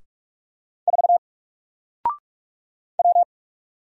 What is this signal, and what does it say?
Morse code, heard as a keyed tone, about 47 wpm, 700 Hz. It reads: 4 W
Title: Single Letter-Number Alt 45wpm